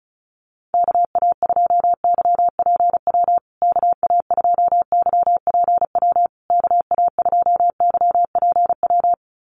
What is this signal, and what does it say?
Morse code, heard as a keyed tone, about 34 wpm, 700 Hz. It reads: KA2YPW KA2YPW KA2YPW
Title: USA Call Signs Repeated 3x 35wpm